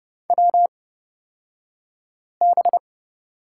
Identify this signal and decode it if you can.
Morse code, heard as a keyed tone, about 29 wpm, spaced as for 5 wpm, 700 Hz. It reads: WB
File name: Lesson 36d - BT - Two Characters 30wpm